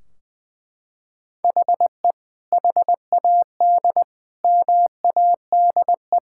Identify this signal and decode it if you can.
Morse code, heard as a keyed tone, about 20 wpm, 700 Hz. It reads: HE HAD MADE
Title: Sentences from Top 1600 Words 20wpm